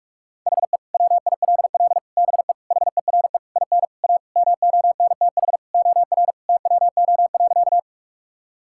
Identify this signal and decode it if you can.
Morse code, heard as a keyed tone, about 44 wpm, 700 Hz. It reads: HE WILL BE HERE IN A MONTH OR TWO.
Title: Sentences from Top 1000 Words - Review Part 5 - 45wpm